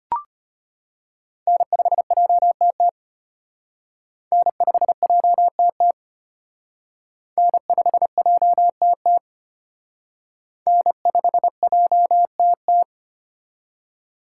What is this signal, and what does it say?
Morse code, heard as a keyed tone, about 33 wpm, 700 Hz. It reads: N5JTT N5JTT N5JTT N5JTT